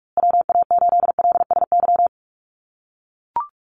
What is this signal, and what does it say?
Morse code, heard as a keyed tone, about 44 wpm, 700 Hz. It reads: WA8LSY
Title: USA Call Signs Repeated 3x 45wpm